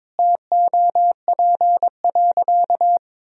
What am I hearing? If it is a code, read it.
Morse code, heard as a keyed tone, about 22 wpm, 700 Hz. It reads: TOP.
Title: Sentences from Top 1000 Words - Review Part 2 - 22wpm